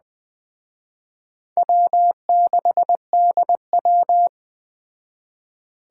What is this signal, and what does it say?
Morse code, heard as a keyed tone, about 20 wpm, 700 Hz. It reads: W6DW